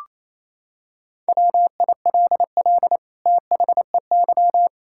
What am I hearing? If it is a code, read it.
Morse code, heard as a keyed tone, about 28 wpm, 700 Hz. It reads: WILL THEY
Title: Sentences from Top 1600 Words 28wpm